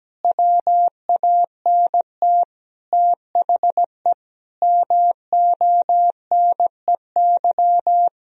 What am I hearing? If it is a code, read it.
Morse code, heard as a keyed tone, about 17 wpm, 700 Hz. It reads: WANT THE MONEY
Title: Sentences from Top 400 Words and Abbreviations 17wpm